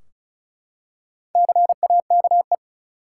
Morse code, heard as a keyed tone, about 36 wpm, 700 Hz. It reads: CAKE
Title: Four Letter Words - 35wpm